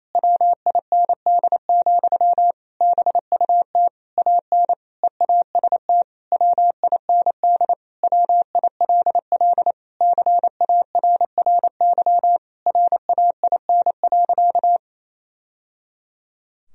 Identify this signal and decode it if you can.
Morse code, heard as a keyed tone, about 28 wpm, 700 Hz. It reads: WIND, BUT AN EAST WIND WILL CARRY RAIN.
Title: Sentences from Top 1000 Words - Review Part 3 - 28wpm